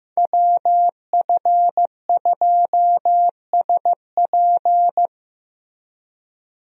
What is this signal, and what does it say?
Morse code, heard as a keyed tone, about 15 wpm, 700 Hz. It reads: WF2SP
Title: USA Call Signs - 5 Characters - Repeated 3x - 15wpm